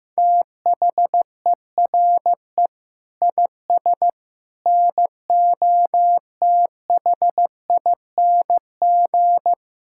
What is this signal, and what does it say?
Morse code, heard as a keyed tone, about 15 wpm, 700 Hz. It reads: THERE IS NOTHING
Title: Sentences from Top 1000 Words - Review Part 6 - 15wpm